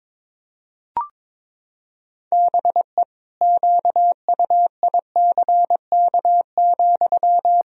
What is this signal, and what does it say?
Morse code, heard as a keyed tone, about 22 wpm, 700 Hz. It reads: BE QUICK,
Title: Sentences from Top 500 Words and Abbreviations 22wpm